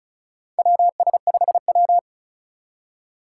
Morse code, heard as a keyed tone, about 36 wpm, 700 Hz. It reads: WS5W
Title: Four Challenging Letter-Number Combinations 35wpm